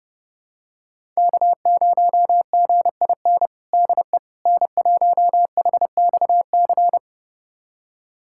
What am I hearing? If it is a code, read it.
Morse code, heard as a keyed tone, about 30 wpm, 700 Hz. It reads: K0GIN DE N1HXC
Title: USA Callsign DE Callsign - 5 Characters - 30wpm